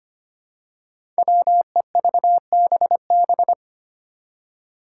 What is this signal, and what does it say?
Morse code, heard as a keyed tone, about 25 wpm, 700 Hz. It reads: WEVBB
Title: Five Challenging Letter-Number Combinations 25wpm